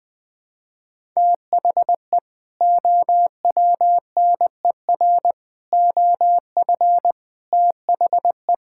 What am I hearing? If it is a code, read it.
Morse code, heard as a keyed tone, about 20 wpm, 700 Hz. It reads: THE OWNER OF THE
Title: Sentences with 3-Word Phrases - 20wpm